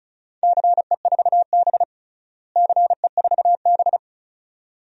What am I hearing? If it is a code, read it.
Morse code, heard as a keyed tone, about 36 wpm, 700 Hz. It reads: CE4B CE4B
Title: DXCC-100 Call Signs - 4 Characters - Repeated 2x - 35wpm